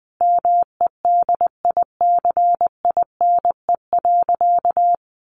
Morse code, heard as a keyed tone, about 20 wpm, 700 Hz. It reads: MEDICINE.
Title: Sentences from Top 900 Words 20wpm